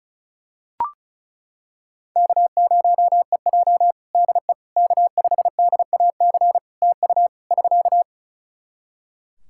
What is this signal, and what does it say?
Morse code, heard as a keyed tone, about 34 wpm, 700 Hz. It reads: K0EJ DE K5DAC TU <SK>
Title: Practice QSO - Exchange 4 - Ending - 35wpm